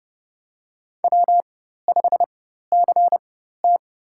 Morse code, heard as a keyed tone, about 30 wpm, spaced as for 14 wpm, 700 Hz. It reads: W5CT
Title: USA Call Signs - 4 Characters - 4x Character Spacing - 30wpm